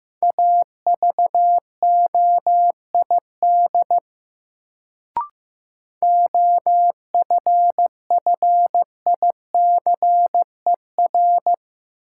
Morse code, heard as a keyed tone, about 15 wpm, 700 Hz. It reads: AVOID OFFICER
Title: Sentences from Top 1200 Words - New Word Drill Alt - 15wpm